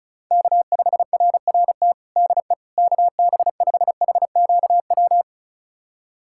Morse code, heard as a keyed tone, about 34 wpm, 700 Hz. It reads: K5RRT DE KB5HQW
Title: USA Callsign DE Callsign 35wpm